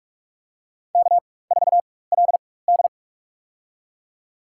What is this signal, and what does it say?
Morse code, heard as a keyed tone, about 47 wpm, spaced as for 21 wpm, 700 Hz. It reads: K4LD
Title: USA Call Signs - 4 Characters - 4x Character Spacing - 45wpm